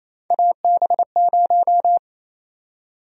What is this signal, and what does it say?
Morse code, heard as a keyed tone, about 28 wpm, 700 Hz. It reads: AB0
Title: USA Call Sign Prefixes - 28wpm